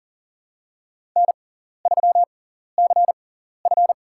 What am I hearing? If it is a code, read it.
Morse code, heard as a keyed tone, about 40 wpm, spaced as for 14 wpm, 700 Hz. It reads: N3CF
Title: USA Call Signs - 4 Characters - 6x Character Spacing - 40wpm